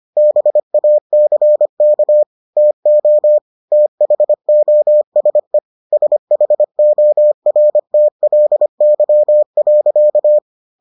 Morse code, heard as a keyed tone, about 25 wpm, 600 Hz. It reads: BACK TO THOSE SHORTLY.